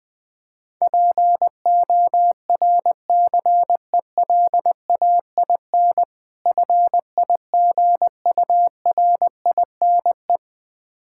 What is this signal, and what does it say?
Morse code, heard as a keyed tone, about 20 wpm, 700 Hz. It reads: PORCELAIN FIGURINE
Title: Difficult Words with Hints - 20wpm